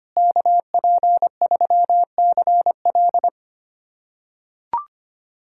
Morse code, heard as a keyed tone, about 25 wpm, 700 Hz. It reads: KP3CL
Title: USA Callsign DE Callsign - 5 Characters - 25wpm